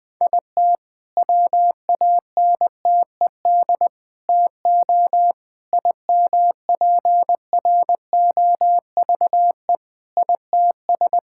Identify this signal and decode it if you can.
Morse code, heard as a keyed tone, about 20 wpm, 700 Hz. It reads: IT WANTED TO IMPROVE ITS
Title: All-Age Appropriate Jokes - 20wpm